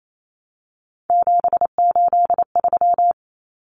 Morse code, heard as a keyed tone, about 28 wpm, 700 Hz. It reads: 783
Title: Numbers - 3-Digit 28wpm